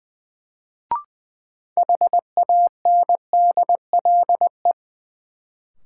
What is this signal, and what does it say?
Morse code, heard as a keyed tone, about 20 wpm, 700 Hz. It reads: HANDLE
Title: Sentences from Top 1100 Words - New Word Drill Alt - 20wpm